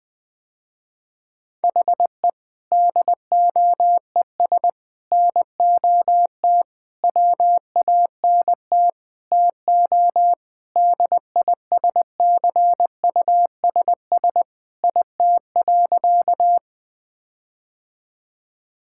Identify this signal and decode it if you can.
Morse code, heard as a keyed tone, about 21 wpm, 700 Hz. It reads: HE DOES NOT WANT TO DISCUSS IT.
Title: Sentences with 5-Word Phrases - Spoken Beginning - 20wpm